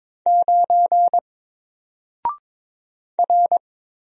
Morse code, heard as a keyed tone, about 22 wpm, 700 Hz. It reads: 9 R
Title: Single Letter-Number Alt 22wpm